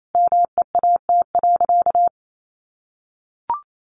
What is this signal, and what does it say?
Morse code, heard as a keyed tone, about 28 wpm, 700 Hz. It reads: MEAT.
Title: Sentences from Top 1200 Words 28wpm